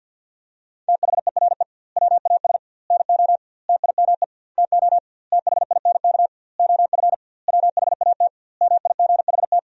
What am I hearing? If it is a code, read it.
Morse code, heard as a keyed tone, about 50 wpm, 700 Hz. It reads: THERE WAS NO TIME TO THINK OF WHAT MIGHT
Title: 2x Word-Spacing - Sentences from Top 400 Words - 50wpm